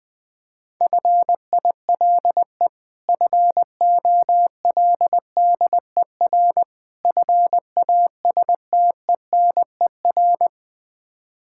Morse code, heard as a keyed tone, about 20 wpm, 700 Hz. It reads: FILE FOLDER FASTENER
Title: Difficult Words - Speed-Racing with Hints - 20wpm